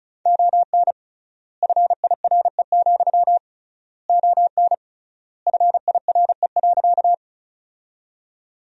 Morse code, heard as a keyed tone, about 34 wpm, 700 Hz. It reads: ON FIRE, ON FIRE.
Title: 3x Word-Spacing - Sentences from Top 400 Words - 35wpm